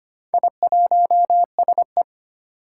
Morse code, heard as a keyed tone, about 25 wpm, 700 Hz. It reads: I1SE
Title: Four Challenging Letter-Number Combinations 25wpm